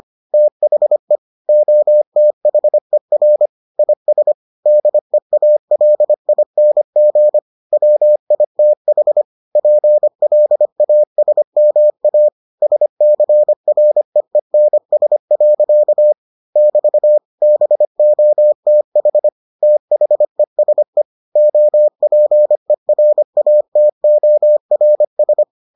Morse code, heard as a keyed tone, about 25 wpm, 600 Hz. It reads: THE OTHER IS DEALING WITH PLASMA SCREENS. <BT> BOTH THESE OPERATORS